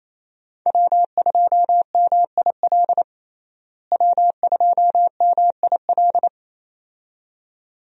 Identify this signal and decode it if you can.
Morse code, heard as a keyed tone, about 28 wpm, 700 Hz. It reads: W2MIL W2MIL
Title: USA Call Signs - 5 Characters - Repeated 2x - 28wpm